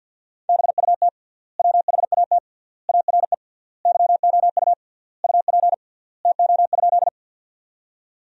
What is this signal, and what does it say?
Morse code, heard as a keyed tone, about 50 wpm, 700 Hz. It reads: BUT WHAT ARE YOU UP TO?
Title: 3x Word-Spacing - Sentences from Top 100 Words - 50wpm